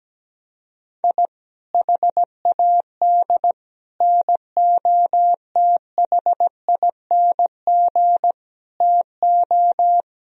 Morse code, heard as a keyed tone, about 17 wpm, 700 Hz. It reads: I HAD NOTHING TO
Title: Sentences from Top 1000 Words - Review Part 1 - 17wpm